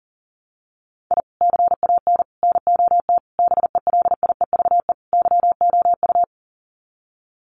Morse code, heard as a keyed tone, about 37 wpm, 700 Hz. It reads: I CAN NOT BELIEVE YOU
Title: Sets of 5 Words Encore 40wpm